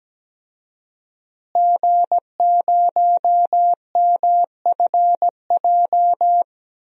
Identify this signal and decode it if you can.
Morse code, heard as a keyed tone, about 17 wpm, 700 Hz. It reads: G0MFJ
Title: DXCC-100 Call Signs - 5 Characters - Repeated 2x - 17wpm